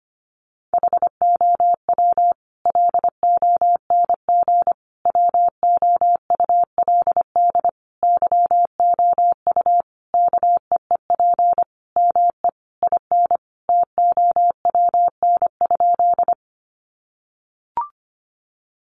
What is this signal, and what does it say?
Morse code, heard as a keyed tone, about 25 wpm, 700 Hz. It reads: HOW LONG WOULD YOU KEEP ME IN TOWN?
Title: Sentences from Top 500 Words 25wpm